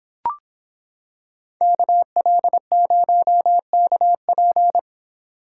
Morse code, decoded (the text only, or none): KL0KP